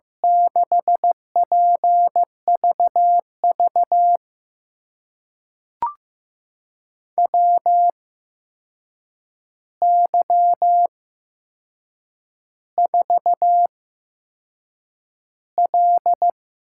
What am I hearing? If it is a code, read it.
Morse code, heard as a keyed tone, about 15 wpm, spaced as for 6 wpm, 700 Hz. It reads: 6PVV WY4L
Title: USA Call Signs - 8x Character Spacing - 15wpm